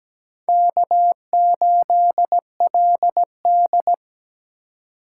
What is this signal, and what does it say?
Morse code, heard as a keyed tone, about 20 wpm, 700 Hz. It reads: K8LD
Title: Speed Racing - USA 1x2 Call Signs 17wpm